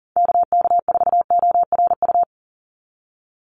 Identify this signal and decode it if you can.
Morse code, heard as a keyed tone, about 40 wpm, 700 Hz. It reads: KK4ORU
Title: USA Call Signs 40wpm